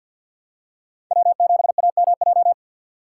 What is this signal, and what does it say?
Morse code, heard as a keyed tone, about 53 wpm, 700 Hz. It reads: W7AMJ
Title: USA Call Signs - 5 Characters - 4x Character Spacing - 50wpm